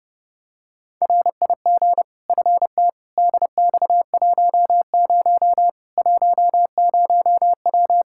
Morse code, heard as a keyed tone, about 30 wpm, 700 Hz. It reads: RIG FT DX10 10W
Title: Practice QSO - Exchange 2 - Info - 30wpm